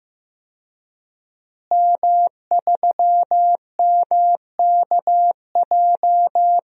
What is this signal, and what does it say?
Morse code, heard as a keyed tone, about 15 wpm, 700 Hz. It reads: M3MKJ
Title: DXCC-100 Call Signs - 5 Characters - Repeated 3x - 15wpm